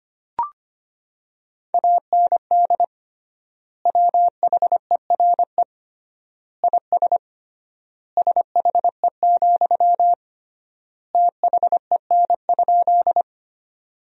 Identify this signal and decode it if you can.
Morse code, heard as a keyed tone, about 25 wpm, 700 Hz. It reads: AND WHERE IS SHE, THEN?
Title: 3x Word-Spacing - Sentences from Top 200 Words - 25wpm